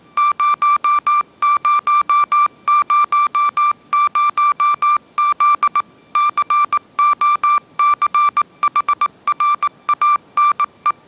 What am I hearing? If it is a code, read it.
Morse code, heard as a keyed tone, about 19 wpm, 1180 Hz. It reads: 0000Z COCHRANE
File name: mrx083 20wpm